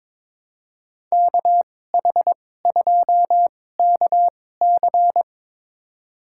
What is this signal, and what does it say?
Morse code, heard as a keyed tone, about 22 wpm, spaced as for 16 wpm, 700 Hz. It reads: KH2KC
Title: USA Call Signs - 2x Character Spacing - 22wpm